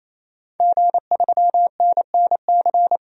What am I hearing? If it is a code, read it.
Morse code, heard as a keyed tone, about 28 wpm, 700 Hz. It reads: G3NNC
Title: DXCC-100 Call Signs - 5 Characters - 28wpm